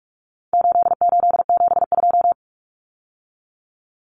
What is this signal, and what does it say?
Morse code, heard as a keyed tone, about 44 wpm, 700 Hz. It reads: 8872
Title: Numbers - 4-Digit 45wpm